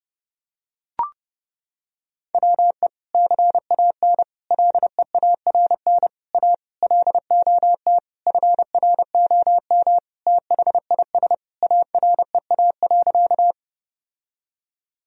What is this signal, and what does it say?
Morse code, heard as a keyed tone, about 32 wpm, 700 Hz. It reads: WE CAN LEARN A LOT FROM THIS AREA.
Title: Sentences from Top 1000 Words - Review Part 2 - 30wpm